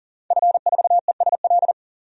Morse code, heard as a keyed tone, about 40 wpm, 700 Hz. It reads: F4ESL
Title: DXCC-100 Call Signs - 5 Characters - 40wpm